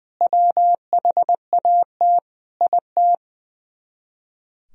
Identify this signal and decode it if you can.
Morse code, heard as a keyed tone, about 21 wpm, 700 Hz. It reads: WHAT IT
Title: Speed-Racing - Sets of 5 Words 20wpm